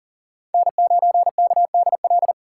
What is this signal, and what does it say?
Morse code, heard as a keyed tone, about 40 wpm, 700 Hz. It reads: N9KDL
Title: USA Call Signs - 3x Character Spacing - 40wpm